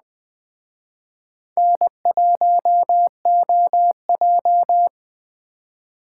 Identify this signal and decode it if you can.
Morse code, heard as a keyed tone, about 20 wpm, 700 Hz. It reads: N1OJ
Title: Speed Racing - USA 1x2 Call Signs 17wpm